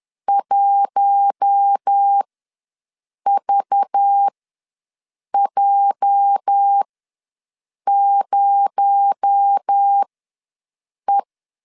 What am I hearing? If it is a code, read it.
Morse code, heard as a keyed tone, about 10 wpm, spaced as for 6 wpm, 800 Hz. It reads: 1VJ0E
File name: LEZIONI / 8 YJ45 / 8 YJ ws 10wpm 4043